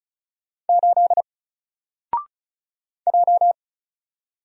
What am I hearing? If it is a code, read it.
Morse code, heard as a keyed tone, about 36 wpm, 700 Hz. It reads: 8 J
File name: Single-Letter-Number-35wpm